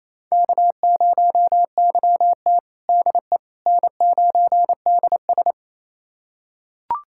Morse code, heard as a keyed tone, about 28 wpm, 700 Hz. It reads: K0YT DE N9DS
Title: USA Callsign DE Callsign - 4 Characters - 28wpm